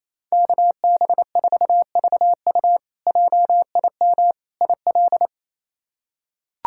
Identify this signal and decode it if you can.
Morse code, heard as a keyed tone, about 28 wpm, 700 Hz. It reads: KB4VU JIM IL
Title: K1USN SST QSO Practice - 28wpm